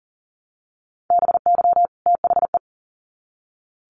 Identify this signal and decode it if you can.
Morse code, heard as a keyed tone, about 40 wpm, 700 Hz. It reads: BY THE
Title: Sets of 2 Words 40wpm